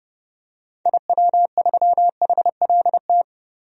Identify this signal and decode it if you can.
Morse code, heard as a keyed tone, about 30 wpm, 700 Hz. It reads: IW3HLT
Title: DXCC-100 Call Signs - 6 Characters - Repeated 3x - 30wpm